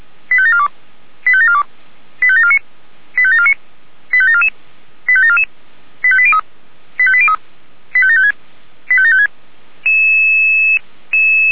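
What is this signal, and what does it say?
Five-tone selective call (ZVEI), heard as quick runs of five tones, tones of about 70 ms. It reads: 85742 85742 85749 85749 85744 85744 85803 85803 85756 85756
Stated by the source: ZVEI five-tone selective call received by SDR.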